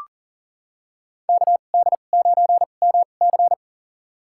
Morse code, heard as a keyed tone, about 40 wpm, spaced as for 29 wpm, 700 Hz. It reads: KD9MC
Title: USA Call Signs - 5 Characters - 2x Character Spacing - 40wpm